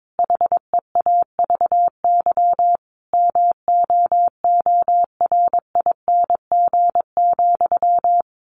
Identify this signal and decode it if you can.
Morse code, heard as a keyed tone, about 22 wpm, 700 Hz. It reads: HEAVY MOORING,
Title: A Sound Of Thunder 22wpm